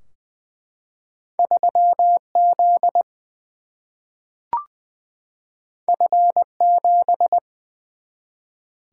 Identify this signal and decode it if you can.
Morse code, heard as a keyed tone, about 20 wpm, 700 Hz. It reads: 3Z F7
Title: Single-Letter Single-Number 20wpm